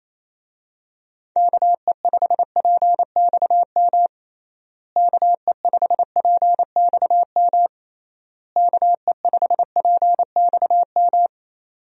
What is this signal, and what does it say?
Morse code, heard as a keyed tone, about 28 wpm, 700 Hz. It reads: KE5PXM KE5PXM KE5PXM
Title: USA Call Signs - 6 Characters - Repeated 3x - 28wpm